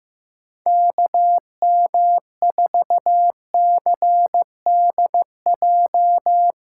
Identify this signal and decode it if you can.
Morse code, heard as a keyed tone, about 16 wpm, 700 Hz. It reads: KM4CDJ
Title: USA Callsign DE Callsign - 6 Characters - 15wpm